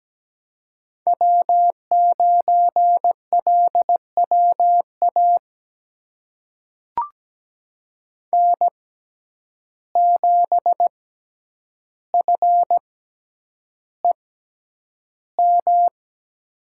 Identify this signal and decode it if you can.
Morse code, heard as a keyed tone, about 17 wpm, spaced as for 9 wpm, 700 Hz. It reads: W9LWA N7FEM
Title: USA Call Signs - 5 Characters - 6x Character Spacing - 17wpm